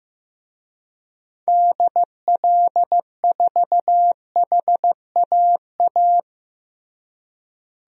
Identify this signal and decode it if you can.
Morse code, heard as a keyed tone, about 15 wpm, 700 Hz. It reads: DL4HAA
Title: DXCC-100 Call Signs - 6 Characters - Repeated 3x - 15wpm